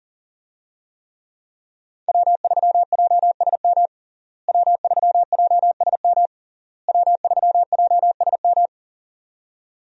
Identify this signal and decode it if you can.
Morse code, heard as a keyed tone, about 40 wpm, 700 Hz. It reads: W3JSM W3JSM W3JSM
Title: USA Call Signs - 5 Characters - Repeated 3x - 40wpm